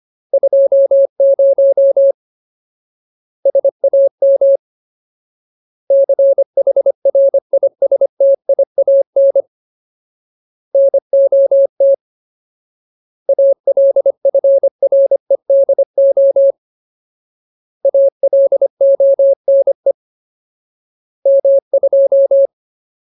Morse code, heard as a keyed tone, about 25 wpm, 550 Hz. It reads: 20 SAM CHRISTIAN NOT ALFREDO ALONE M2
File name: Ultimate Word List / 25WPM Ultimate